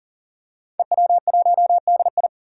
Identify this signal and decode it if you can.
Morse code, heard as a keyed tone, about 37 wpm, 700 Hz. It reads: EW1DI
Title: Five Challenging Letter-Number Combinations 40wpm